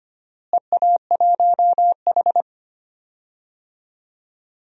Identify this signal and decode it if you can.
Morse code, heard as a keyed tone, about 25 wpm, 700 Hz. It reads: EA1H
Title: DXCC-100 Call Signs - 4 Characters - Repeated 3x - 25wpm